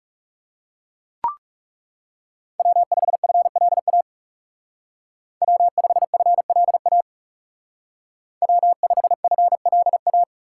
Morse code, heard as a keyed tone, about 36 wpm, 700 Hz. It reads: W5FLA W5FLA W5FLA
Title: Speed Racing - USA 1x3 Call Signs 30wpm